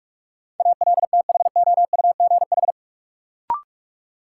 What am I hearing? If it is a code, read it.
Morse code, heard as a keyed tone, about 41 wpm, 700 Hz. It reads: ALTHOUGH
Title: Sentences from Top 600 Words - New Word Drill - 45wpm